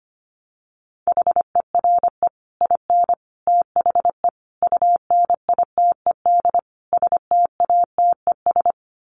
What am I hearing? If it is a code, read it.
Morse code, heard as a keyed tone, about 25 wpm, 700 Hz. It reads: HERE IN THE UNITED STATES
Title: Sets of 5 Words 25wpm